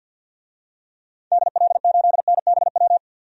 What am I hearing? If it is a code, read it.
Morse code, heard as a keyed tone, about 52 wpm, 700 Hz. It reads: DL8NBW
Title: DXCC-100 Call Signs - 6 Characters - 50wpm